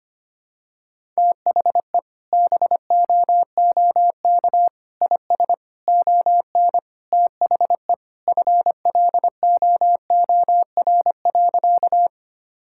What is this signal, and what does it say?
Morse code, heard as a keyed tone, about 25 wpm, 700 Hz. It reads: THE BOOK IS ON THE FLOOR.
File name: practice / 25wpm / Sentences with 3-Word Phrases - 25wpm